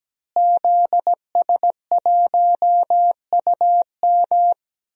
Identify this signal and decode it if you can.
Morse code, heard as a keyed tone, about 17 wpm, 700 Hz. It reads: ZS1UM
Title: DXCC-100 Call Signs - 5 Characters - Repeated 3x - 17wpm